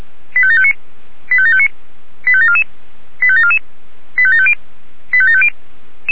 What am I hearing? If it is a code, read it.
Five-tone selective call (ZVEI), heard as quick runs of five tones, tones of about 70 ms. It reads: 85759 85759 85744 85744 85750 85750
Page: SDR/ZVEI